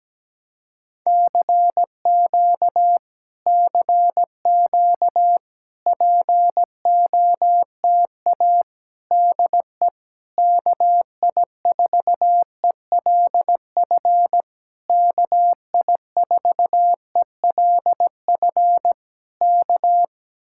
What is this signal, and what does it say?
Morse code, heard as a keyed tone, about 17 wpm, 700 Hz. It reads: CQ CQ POTA DE KI4ELF KI4ELF K
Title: Speed-Racing Common QSO Phrases - 17wpm